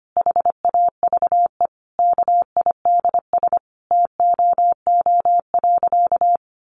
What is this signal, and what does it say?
Morse code, heard as a keyed tone, about 25 wpm, 700 Hz. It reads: HAVE KIDS TOO.
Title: Easy Phrases 25wpm